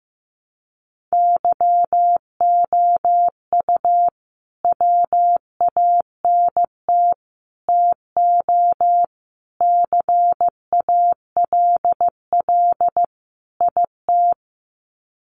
Morse code, heard as a keyed tone, about 15 wpm, 700 Hz. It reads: YOU WANT TO CALL IT
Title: Sets of 5 Words Encore2 15wpm